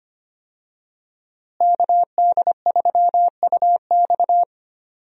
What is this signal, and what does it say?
Morse code, heard as a keyed tone, about 25 wpm, 700 Hz. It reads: KD3UX
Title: USA Call Signs - 5 Characters - Repeated 3x - 25wpm